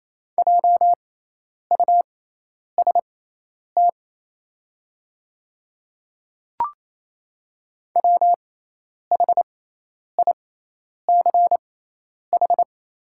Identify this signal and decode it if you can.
Morse code, heard as a keyed tone, about 28 wpm, spaced as for 10 wpm, 690 Hz. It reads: JUST WHICH
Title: 6x Character Spacing - Top 200 Words - 28wpm